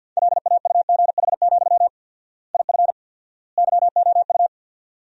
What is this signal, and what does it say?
Morse code, heard as a keyed tone, about 51 wpm, 700 Hz. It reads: LAUGH, IF YOU
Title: 4x Word-Spacing - Sentences from Top 500 Words - 50wpm